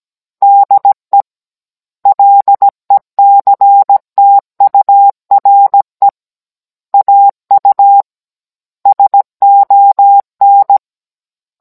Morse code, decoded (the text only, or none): DE LECTURE AU SON